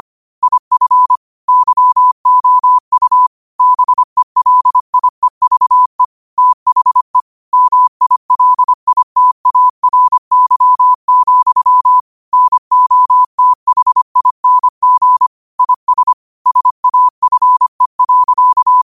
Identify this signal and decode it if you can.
Morse code, heard as a keyed tone, about 25 wpm, 1000 Hz. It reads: IF YOU BELIEVE THE MILITARY, NOTHING IS SAFE.